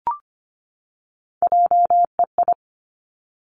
Morse code, heard as a keyed tone, about 25 wpm, 700 Hz. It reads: JEI
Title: Three Challenging Letter-Number Combinations 25wpm